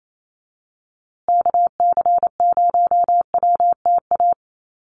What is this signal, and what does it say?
Morse code, heard as a keyed tone, about 28 wpm, 700 Hz. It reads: KC0WTA
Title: USA Call Signs Repeated 3x 28wpm